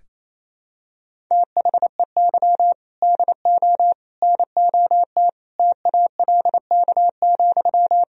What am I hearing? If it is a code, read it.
Morse code, heard as a keyed tone, about 27 wpm, 700 Hz. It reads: THEY DO NOT TALK,
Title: Sentences from Top 1000 Words - Review Part 1 - 28wpm